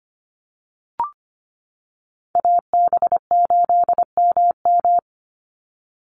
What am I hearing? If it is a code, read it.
Morse code, heard as a keyed tone, about 25 wpm, 700 Hz. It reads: AB8MM
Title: USA Call Signs 25wpm